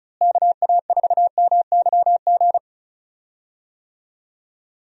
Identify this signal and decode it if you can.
Morse code, heard as a keyed tone, about 36 wpm, 700 Hz. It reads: KA4MYG
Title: USA Call Signs - 6 Characters - Repeated 3x - 35wpm